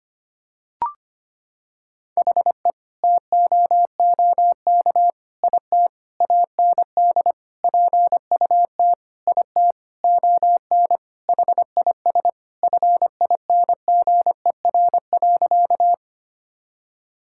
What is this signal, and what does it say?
Morse code, heard as a keyed tone, about 25 wpm, 700 Hz. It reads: HE TOOK IT AND PUT IT ON HIS FINGER.
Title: Sentences from Top 1600 Words 25wpm